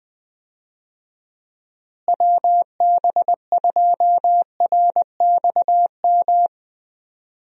Morse code, heard as a keyed tone, about 20 wpm, 700 Hz. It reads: WB2RXM
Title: USA Call Signs - 6 Characters - Repeated 2x - 20wpm